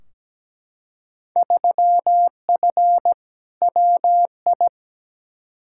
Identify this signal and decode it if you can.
Morse code, heard as a keyed tone, about 17 wpm, 700 Hz. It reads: 3F WI
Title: Field Day Exchanges - 17wpm